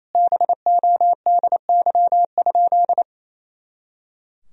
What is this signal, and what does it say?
Morse code, heard as a keyed tone, about 28 wpm, 700 Hz. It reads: BODY?
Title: Sentences from Top 400 Words and Abbreviations 28wpm